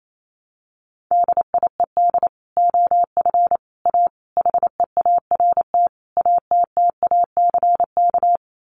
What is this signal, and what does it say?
Morse code, heard as a keyed tone, about 28 wpm, 700 Hz. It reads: DIED OF A HEART ATTACK
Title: Sets of 5 Words Encore2 28wpm